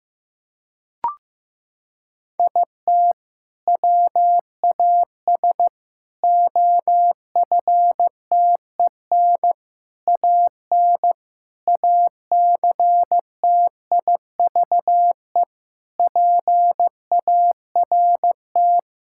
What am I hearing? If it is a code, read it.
Morse code, heard as a keyed tone, about 15 wpm, 700 Hz. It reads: IT WAS OFTEN AN ACTIVE PART
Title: Sentences from Top 1000 Words - Review Part 6 - 15wpm